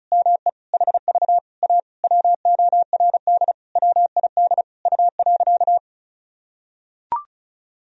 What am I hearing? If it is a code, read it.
Morse code, heard as a keyed tone, about 34 wpm, 700 Hz. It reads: ME HV A WORD WID U.
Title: Sentences from Top 300 Words and Abbreviations 35wpm